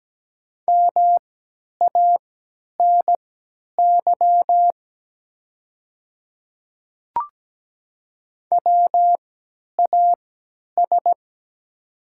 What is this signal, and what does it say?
Morse code, heard as a keyed tone, about 17 wpm, spaced as for 10 wpm, 700 Hz. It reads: MANY WAS
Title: 3x Character Spacing - Top 100 Words - 17wpm